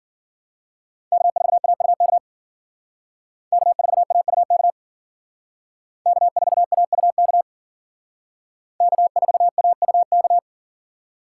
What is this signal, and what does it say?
Morse code, heard as a keyed tone, about 52 wpm, 700 Hz. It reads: K4AUK K4AUK K4AUK K4AUK